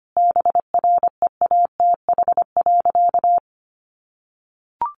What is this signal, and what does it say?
Morse code, heard as a keyed tone, about 25 wpm, 700 Hz. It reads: BREATH.
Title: Additional Easy Phrases 25wpm